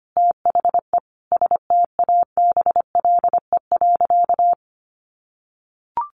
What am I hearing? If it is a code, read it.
Morse code, heard as a keyed tone, about 25 wpm, 700 Hz. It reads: THE STABLE.
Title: Sentences from Top 1500 Words 25wpm